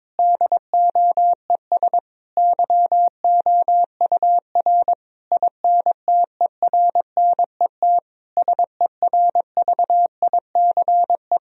Morse code, heard as a keyed tone, about 22 wpm, 700 Hz. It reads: DOES YOUR INTERNET SERVICE
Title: Sentences from Top 1000 Words - Review Part 4 - 22wpm